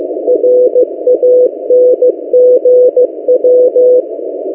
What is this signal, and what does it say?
Morse code, heard as a keyed tone, about 15 wpm, 500 Hz. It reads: RANGW
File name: M08a- -1355utc-8096khz